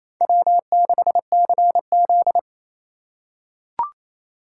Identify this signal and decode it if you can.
Morse code, heard as a keyed tone, about 28 wpm, 700 Hz. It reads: W6CZ
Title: USA Call Signs - 4 Characters - 4x Character Spacing - 28wpm